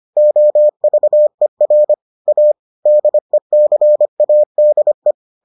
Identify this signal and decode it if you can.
Morse code, heard as a keyed tone, about 25 wpm, 600 Hz. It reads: OVER A DECADE